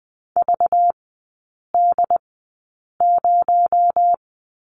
Morse code, heard as a keyed tone, about 20 wpm, spaced as for 8 wpm, 700 Hz. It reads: VD0
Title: 218 Lesson 35e - Number 0 - Three Characters 20wpm